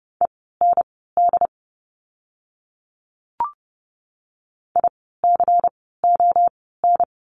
Morse code, heard as a keyed tone, about 30 wpm, spaced as for 17 wpm, 700 Hz. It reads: END ICON
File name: Lesson 16f - Letter c - Words 30wpm